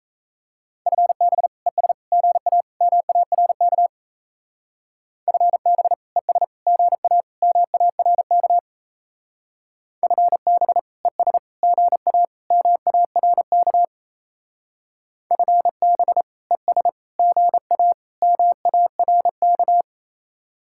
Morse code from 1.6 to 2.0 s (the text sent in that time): ES